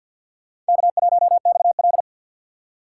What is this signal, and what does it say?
Morse code, heard as a keyed tone, about 46 wpm, 700 Hz. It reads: K1XL
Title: USA Call Signs - 4 Characters - 6x Character Spacing - 50wpm